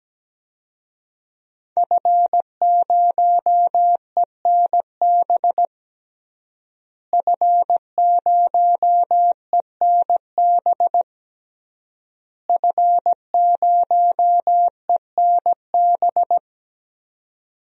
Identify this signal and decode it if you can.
Morse code, heard as a keyed tone, about 17 wpm, 700 Hz. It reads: F0ENB F0ENB F0ENB
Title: DXCC-100 Call Signs - 5 Characters - Repeated 3x - 17wpm